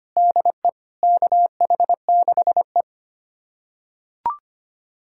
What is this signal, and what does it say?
Morse code, heard as a keyed tone, about 25 wpm, 700 Hz. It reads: DE KH6E
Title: USA Callsign DE Callsign - 4 Characters - 25wpm